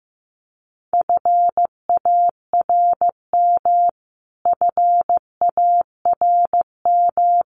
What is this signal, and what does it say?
Morse code, heard as a keyed tone, about 15 wpm, 700 Hz. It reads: FARM FARM
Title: Top 500 Words Repeated 15wpm